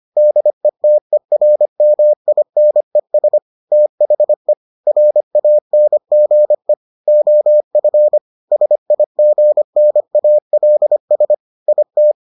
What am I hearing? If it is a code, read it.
Morse code, heard as a keyed tone, about 25 wpm, 600 Hz. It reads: DETERMINES THE RANGE OF SIGNALS IT